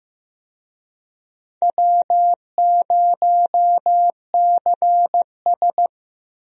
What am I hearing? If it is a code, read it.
Morse code, heard as a keyed tone, about 16 wpm, 700 Hz. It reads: W0CS
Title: USA Call Signs - 4 Characters - Repeated 2x - 15wpm